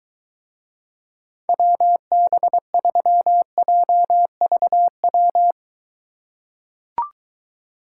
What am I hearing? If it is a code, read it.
Morse code, heard as a keyed tone, about 23 wpm, 700 Hz. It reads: WB3JVW